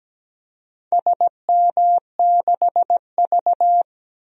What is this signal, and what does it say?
Morse code, heard as a keyed tone, about 17 wpm, 700 Hz. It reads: SM6V
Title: DXCC-100 Call Signs - 4 Characters - Repeated 3x - 17wpm